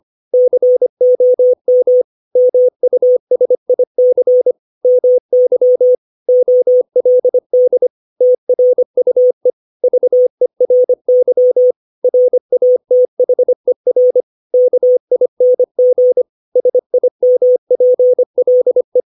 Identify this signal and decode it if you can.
Morse code, heard as a keyed tone, about 25 wpm, 500 Hz. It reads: COM MUSIC MY OLD TRUE VERY RATHER KING SIMPLE
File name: 632 words / 632 Words 25WPM